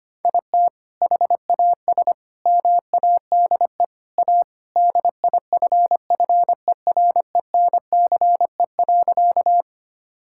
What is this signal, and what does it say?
Morse code, heard as a keyed tone, about 25 wpm, 700 Hz. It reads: IT HAS MADE A DIFFERENCE.
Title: Sentences from Top 1000 Words - Review Part 5 - 25wpm